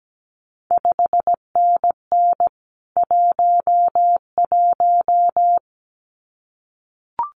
Morse code, heard as a keyed tone, about 17 wpm, 700 Hz. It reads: 5NN 11
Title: CQ WW DX Contest Exchange Encore - 17wpm